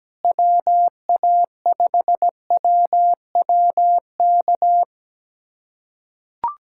Morse code, heard as a keyed tone, about 17 wpm, 700 Hz. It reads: WA5WWK
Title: USA Callsign DE Callsign - 6 Characters - 17wpm